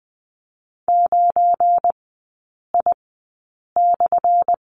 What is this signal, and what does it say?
Morse code, heard as a keyed tone, about 20 wpm, spaced as for 8 wpm, 700 Hz. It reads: 9I/
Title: 178 Lesson 29e - Slash - Three Characters 20wpm